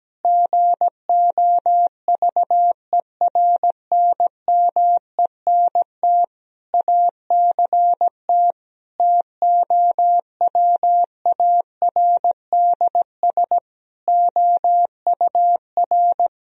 Morse code, heard as a keyed tone, about 17 wpm, 700 Hz. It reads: GOVERNMENT ACT TOWARDS OUR
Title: Sentences from Top 1000 Words - Review Part 4 - 17wpm